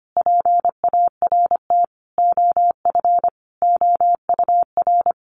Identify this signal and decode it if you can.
Morse code, heard as a keyed tone, about 25 wpm, 700 Hz. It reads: PART OF OUR
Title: Sentences from Top 1400 Words 25wpm